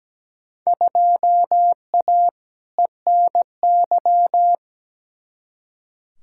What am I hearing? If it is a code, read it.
Morse code, heard as a keyed tone, about 17 wpm, 700 Hz. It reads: 2A ENY
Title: Field Day Exchanges - 17wpm